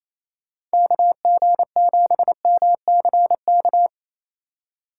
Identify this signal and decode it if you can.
Morse code, heard as a keyed tone, about 35 wpm, 700 Hz. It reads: KG7MCK
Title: Speed Racing - USA 2x3 Call Signs 28wpm